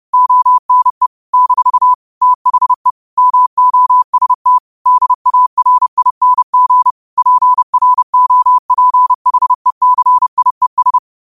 Morse code, heard as a keyed tone, about 30 wpm, 1000 Hz. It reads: ONE - THE MOST DARING PROPHECIES
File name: 30wpm